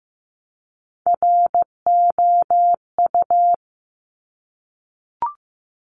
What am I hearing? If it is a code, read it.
Morse code, heard as a keyed tone, about 15 wpm, 700 Hz. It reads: ROU
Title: Top Three Letter Combinations in Words 15wpm